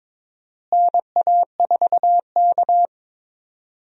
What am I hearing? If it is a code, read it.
Morse code, heard as a keyed tone, about 22 wpm, 700 Hz. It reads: NA4K
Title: USA Call Signs - 4 Characters - Repeated 2x - 22wpm